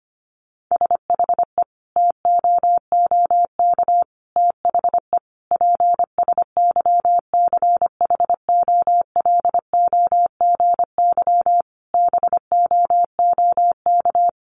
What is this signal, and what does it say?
Morse code, heard as a keyed tone, about 25 wpm, 700 Hz. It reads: SHE TOOK THE PSYCHOLOGY BOOK
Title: Sentences from Top 1400 Words 25wpm